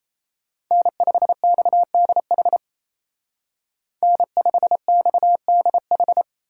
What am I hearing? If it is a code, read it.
Morse code, heard as a keyed tone, about 31 wpm, 700 Hz. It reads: N5XDH N5XDH
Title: Speed Racing - USA 1x3 Call Signs 28wpm